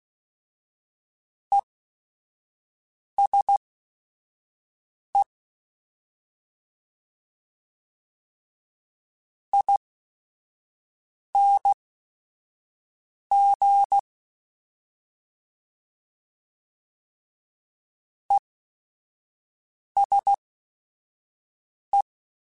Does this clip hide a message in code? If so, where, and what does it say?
Yes: Morse code from 1.5 to 22.0 s, heard as a keyed tone, about 16 wpm, spaced as for 5 wpm, 800 Hz. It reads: ESE ING ESE